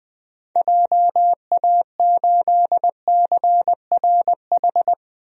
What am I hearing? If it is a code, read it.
Morse code, heard as a keyed tone, about 20 wpm, 700 Hz. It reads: JA8CRH
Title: DXCC-100 Call Signs - 6 Characters - Repeated 3x - 20wpm